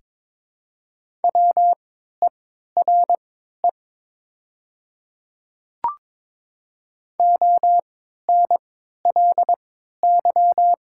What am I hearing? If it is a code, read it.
Morse code, heard as a keyed tone, about 22 wpm, spaced as for 13 wpm, 700 Hz. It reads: WERE ONLY
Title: 3x Character Spacing - Top 500 Words - 22wpm